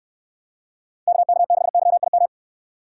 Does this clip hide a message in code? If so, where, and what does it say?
Yes: Morse code from 1.1 to 2.3 s, heard as a keyed tone, about 69 wpm, 700 Hz. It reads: KK6YIM